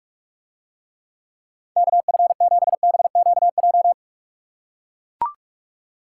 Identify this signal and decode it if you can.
Morse code, heard as a keyed tone, about 44 wpm, 700 Hz. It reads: KF7BQJ